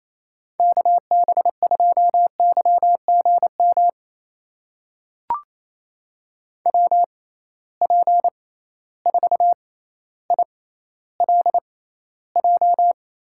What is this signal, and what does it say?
Morse code, heard as a keyed tone, about 28 wpm, spaced as for 14 wpm, 700 Hz. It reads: KB2YGM WP4ILJ
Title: USA Call Signs - 6 Characters - 6x Character Spacing - 28wpm